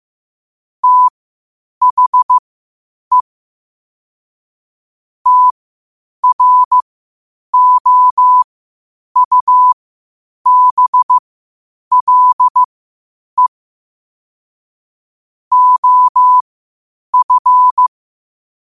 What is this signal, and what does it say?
Morse code, heard as a keyed tone, about 15 wpm, spaced as for 8 wpm, 1000 Hz. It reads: THE TROUBLE OF
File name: qotdfiles / 8wpm